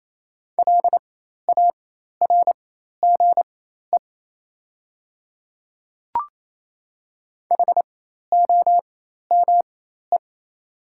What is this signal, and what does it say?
Morse code, heard as a keyed tone, about 27 wpm, spaced as for 13 wpm, 690 Hz. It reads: LARGE HOME
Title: 4x Character Spacing - Top 500 Words - 28wpm